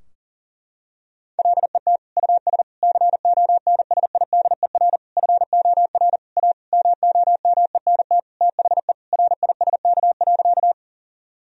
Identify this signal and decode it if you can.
Morse code, heard as a keyed tone, about 40 wpm, 700 Hz. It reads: LET US CONSIDER FOR A MOMENT THE RISK.
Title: Sentences from Top 1000 Words - Review Part 4 - 40wpm